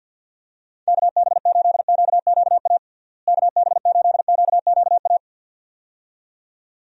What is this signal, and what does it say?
Morse code, heard as a keyed tone, about 52 wpm, 700 Hz. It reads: KB8QQA KB8QQA
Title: USA Call Signs - 6 Characters - Repeated 2x - 50wpm